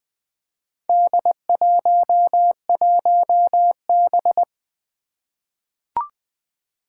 Morse code, heard as a keyed tone, about 20 wpm, 700 Hz. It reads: D11B
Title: Four Challenging Letter-Number Combinations 20wpm